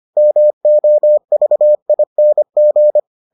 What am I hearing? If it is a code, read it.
Morse code, heard as a keyed tone, about 25 wpm, 600 Hz. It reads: MOVING